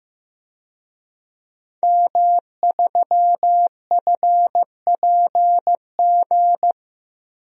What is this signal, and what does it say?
Morse code, heard as a keyed tone, about 15 wpm, 700 Hz. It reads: M3FPG
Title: DXCC-100 Call Signs - 5 Characters - Repeated 3x - 15wpm